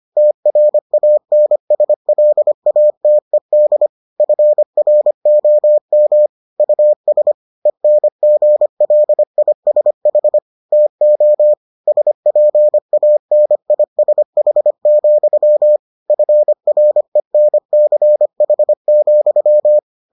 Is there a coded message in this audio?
Yes: Morse code from 0.2 to 19.8 s, heard as a keyed tone, about 25 wpm, 600 Hz. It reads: TRANSLATED FROM US ENGLISH TO SPANISH, FRENCH,